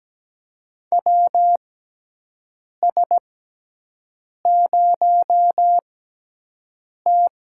Morse code, heard as a keyed tone, about 17 wpm, spaced as for 6 wpm, 700 Hz. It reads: WS0T
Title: USA Call Signs - 4 Characters - 6x Character Spacing - 17wpm